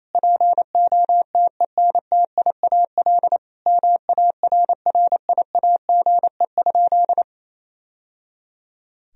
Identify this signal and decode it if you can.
Morse code, heard as a keyed tone, about 28 wpm, 700 Hz. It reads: POTENTIAL MARRIAGE?
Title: Sentences from Top 1000 Words - Review Part 4 - 28wpm